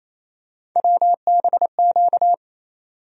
Morse code, heard as a keyed tone, about 28 wpm, 700 Hz. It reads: WBQ
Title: USA Call Sign Postfixes - 3 Letters 28wpm